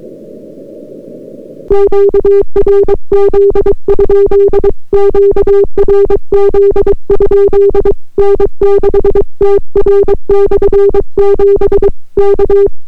Morse code, heard as a keyed tone, about 22 wpm, 400 Hz. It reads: QRZ? QRZ? N6TR/7 K